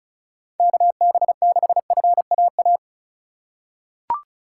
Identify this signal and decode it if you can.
Morse code, heard as a keyed tone, about 36 wpm, 700 Hz. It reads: KB6FAA
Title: USA Call Signs - 6 Characters - 2x Character Spacing - 35wpm